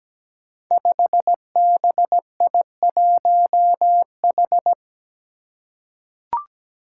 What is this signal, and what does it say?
Morse code, heard as a keyed tone, about 17 wpm, 700 Hz. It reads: 5BI1H
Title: Five Challenging Letter-Number Combinations 17wpm